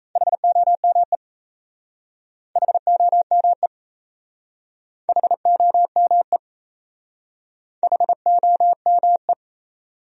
Morse code, heard as a keyed tone, about 37 wpm, 700 Hz. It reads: HOME HOME HOME HOME